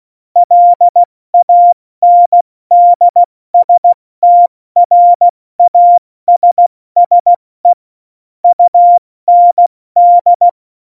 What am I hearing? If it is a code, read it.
Morse code, heard as a keyed tone, about 16 wpm, 700 Hz. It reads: LANDSTRASSE UND
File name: Morsekurs / Witze / Witz 29. Januar 2019, 70 F80